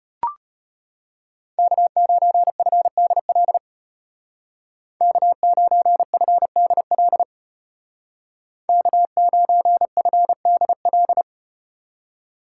Morse code, heard as a keyed tone, about 35 wpm, 700 Hz. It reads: K9FDL K9FDL K9FDL